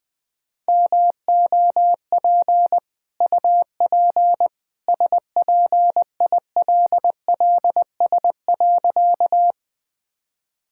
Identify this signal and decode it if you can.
Morse code, heard as a keyed tone, about 21 wpm, 700 Hz. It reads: MOP UP SPILLS.
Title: Easy Sentences with Hints - 20wpm